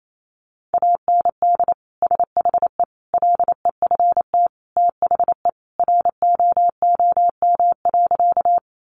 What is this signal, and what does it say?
Morse code, heard as a keyed tone, about 28 wpm, 700 Hz. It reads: AND SHE LEFT THE ROOM.
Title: Sentences from Top 300 Words 28wpm